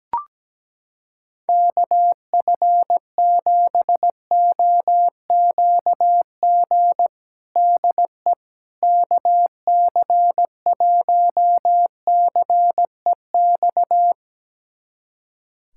Morse code, heard as a keyed tone, about 17 wpm, 700 Hz. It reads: KF7OQG DE KC1CEX
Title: USA Callsign DE Callsign - 6 Characters - 17wpm